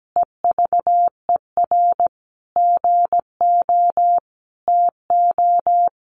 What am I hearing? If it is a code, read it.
Morse code, heard as a keyed tone, about 17 wpm, 700 Hz. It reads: EVER GO TO
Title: Sentences from Top 1300 Words 17wpm